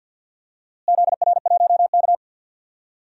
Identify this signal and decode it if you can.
Morse code, heard as a keyed tone, about 49 wpm, 700 Hz. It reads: ZR1K
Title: DXCC-100 Call Signs - 4 Characters - Repeated 2x - 50wpm